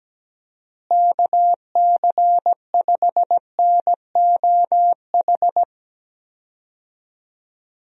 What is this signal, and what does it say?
Morse code, heard as a keyed tone, about 17 wpm, 700 Hz. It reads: KC5NOH
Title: USA Call Signs - 6 Characters - Repeated 3x - 17wpm